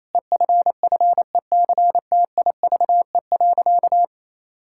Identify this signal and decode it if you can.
Morse code, heard as a keyed tone, about 28 wpm, 700 Hz. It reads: EFFECTIVE.
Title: Sentences from Top 1000 Words - Review Part 6 - 28wpm